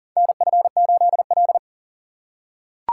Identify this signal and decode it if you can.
Morse code, heard as a keyed tone, about 40 wpm, 700 Hz. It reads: NF8L
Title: USA Call Signs - 4 Characters - 4x Character Spacing - 40wpm